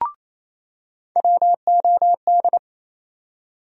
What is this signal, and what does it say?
Morse code, heard as a keyed tone, about 28 wpm, 700 Hz. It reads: WOD
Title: USA Call Sign Postfixes - 3 Letters 28wpm